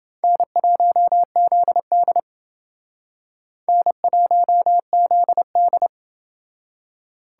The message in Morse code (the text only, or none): N1ZD N1ZD